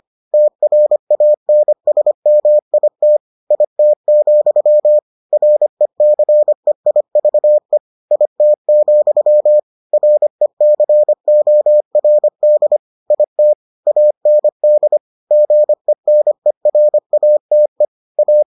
Morse code, heard as a keyed tone, about 25 wpm, 600 Hz. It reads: TRANSMIT IT, RECEIVE IT, RECORD IT AND GENERATE A